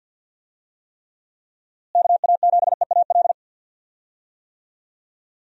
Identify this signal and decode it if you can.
Morse code, heard as a keyed tone, about 50 wpm, 700 Hz. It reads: KA7EAL
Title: USA Call Signs - 6 Characters - Repeated 3x - 50wpm